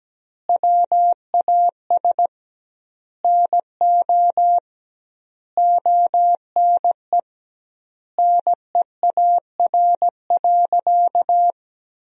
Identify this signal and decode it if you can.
Morse code, heard as a keyed tone, about 17 wpm, 700 Hz. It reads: WAS NO ONE NEAR.
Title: 2x Word-Spacing - Sentences from Top 200 Words - 17wpm